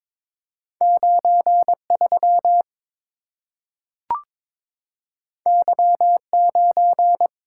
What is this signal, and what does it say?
Morse code, heard as a keyed tone, about 22 wpm, 700 Hz. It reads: 93 Y9
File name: practice / 22wpm / Two Letter-Number Combinations - 22wpm